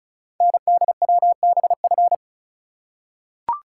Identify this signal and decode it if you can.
Morse code, heard as a keyed tone, about 36 wpm, 690 Hz. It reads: NDWBF
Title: Code Groups - 5 Letters - 35wpm